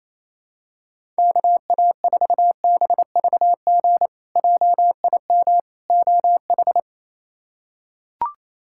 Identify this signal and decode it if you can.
Morse code, heard as a keyed tone, about 28 wpm, 700 Hz. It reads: KA4BVG JIM OH
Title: K1USN SST QSO Practice - 28wpm